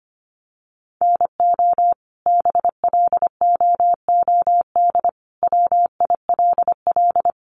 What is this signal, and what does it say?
Morse code, heard as a keyed tone, about 25 wpm, 700 Hz. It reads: NO BLOOD WILL
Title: Sentences from Top 1200 Words 25wpm